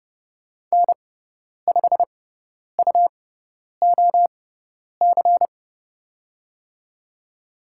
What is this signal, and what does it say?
Morse code, heard as a keyed tone, about 29 wpm, spaced as for 10 wpm, 690 Hz. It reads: N5UOC
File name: Lesson 18g - Letter m - Call Signs 30wpm